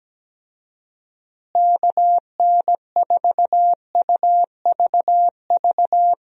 Morse code, heard as a keyed tone, about 17 wpm, 700 Hz. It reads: KN4UVV
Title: USA Call Signs - 6 Characters - Repeated 2x - 17wpm